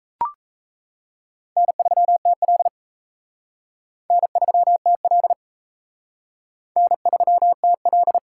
Morse code, heard as a keyed tone, about 40 wpm, 700 Hz. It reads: N3TL N3TL N3TL